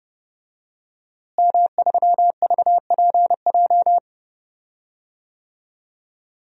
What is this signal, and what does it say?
Morse code, heard as a keyed tone, about 30 wpm, 700 Hz. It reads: M3VPJ
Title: DXCC-100 Call Signs - 5 Characters - Repeated 2x - 30wpm